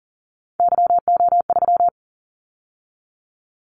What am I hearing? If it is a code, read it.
Morse code, heard as a keyed tone, about 40 wpm, 700 Hz. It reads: YO3
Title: Callsign Prefixes for Top 25 DXCC - 40wpm